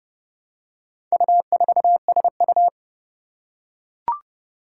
Morse code, heard as a keyed tone, about 30 wpm, 700 Hz. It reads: U4SU
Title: Four Challenging Letter-Number Combinations 30wpm